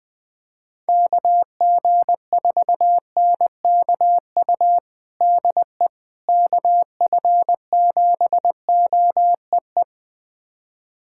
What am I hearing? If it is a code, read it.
Morse code, heard as a keyed tone, about 21 wpm, 700 Hz. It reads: KG4NKU DE KF7OEE
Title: USA Callsign DE Callsign - 6 Characters - 20wpm